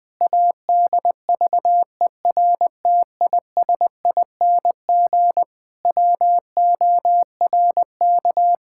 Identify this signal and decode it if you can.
Morse code, heard as a keyed tone, about 21 wpm, 700 Hz. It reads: ADVERTISING WORK
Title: Sentences from Top 1000 Words - Review Part 6 - 20wpm